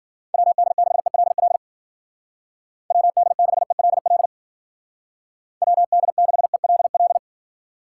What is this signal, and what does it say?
Morse code, heard as a keyed tone, about 50 wpm, 700 Hz. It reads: WD6ELL WD6ELL WD6ELL
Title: Speed Racing - USA 2x3 Call Signs 40wpm